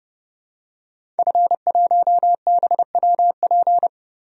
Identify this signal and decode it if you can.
Morse code, heard as a keyed tone, about 30 wpm, 700 Hz. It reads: F1BWP
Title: DXCC-100 Call Signs - 5 Characters - Repeated 2x - 30wpm